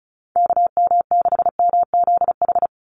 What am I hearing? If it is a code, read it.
Morse code, heard as a keyed tone, about 34 wpm, 700 Hz. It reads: KM6MZH
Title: USA Call Signs 35wpm